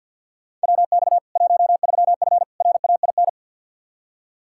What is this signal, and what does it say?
Morse code, heard as a keyed tone, about 50 wpm, 700 Hz. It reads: WX 13F RAIN
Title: Practice QSO - Exchange 2 - Info - 50wpm